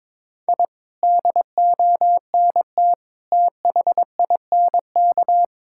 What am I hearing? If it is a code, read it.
Morse code, heard as a keyed tone, about 22 wpm, 700 Hz. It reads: I DONT THINK
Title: Sentences from Top 1000 Words - Review Part 6 - 22wpm